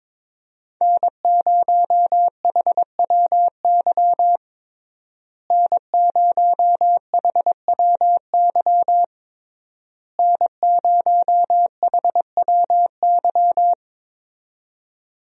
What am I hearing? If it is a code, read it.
Morse code, heard as a keyed tone, about 22 wpm, 700 Hz. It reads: N0HWY N0HWY N0HWY
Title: USA Call Signs - 5 Characters - Repeated 3x - 22wpm